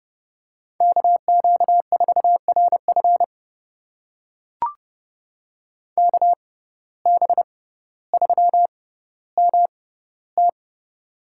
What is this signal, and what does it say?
Morse code, heard as a keyed tone, about 30 wpm, spaced as for 15 wpm, 700 Hz. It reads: KQ4RF KB3MT
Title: USA Call Signs - 5 Characters - 6x Character Spacing - 30wpm